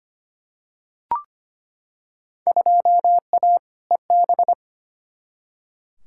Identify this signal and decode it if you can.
Morse code, heard as a keyed tone, about 25 wpm, 700 Hz. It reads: 2A EB
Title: Field Day Exchanges - 25wpm